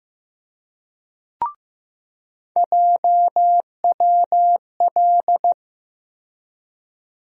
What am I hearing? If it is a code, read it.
Morse code, heard as a keyed tone, about 15 wpm, 700 Hz. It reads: JWL
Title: USA Call Sign Postfixes - 3 Letters 15wpm